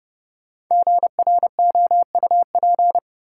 Morse code, heard as a keyed tone, about 30 wpm, 700 Hz. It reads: GROUP
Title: Sentences from Top 1000 Words - Review Part 1 - New Word Drill - 30wpm